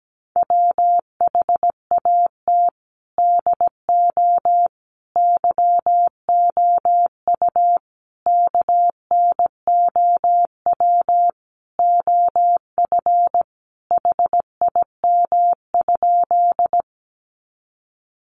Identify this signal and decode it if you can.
Morse code, heard as a keyed tone, about 17 wpm, 700 Hz. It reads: WHAT DO YOU KNOW OF HIM?
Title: Sentences from Top 100 Words 17wpm